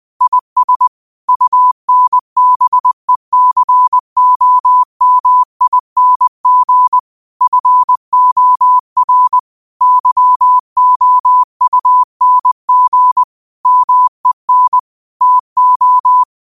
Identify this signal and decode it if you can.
Morse code, heard as a keyed tone, about 20 wpm, 1000 Hz. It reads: IS UNBECOMING FOR YOUNG MEN TO
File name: qotdfiles / 20wpm